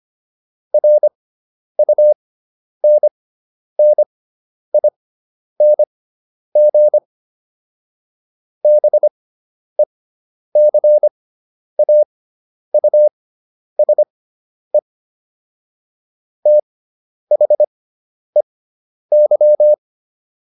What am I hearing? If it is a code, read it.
Morse code, heard as a keyed tone, about 25 wpm, spaced as for 10 wpm, 600 Hz. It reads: RUNNING BECAUSE THEY